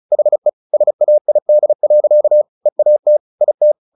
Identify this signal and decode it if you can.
Morse code, heard as a keyed tone, about 34 wpm, 600 Hz. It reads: HE SAID. EAT IT
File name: the old man and the sea / the old man and the sea 1 35wpm